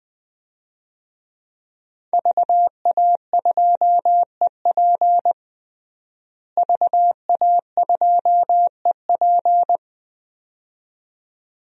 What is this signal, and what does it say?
Morse code, heard as a keyed tone, about 21 wpm, 700 Hz. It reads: VA2EP VA2EP
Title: DXCC-100 Call Signs - 5 Characters - Repeated 2x - 20wpm